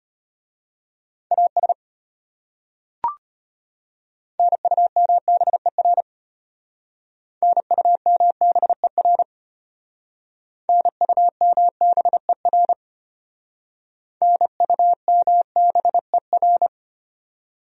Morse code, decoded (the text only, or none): AS NUMBER NUMBER NUMBER NUMBER